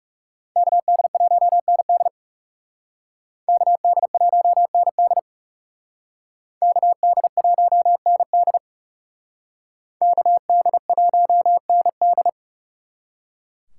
Morse code, heard as a keyed tone, about 38 wpm, 700 Hz. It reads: KD1ND KD1ND KD1ND KD1ND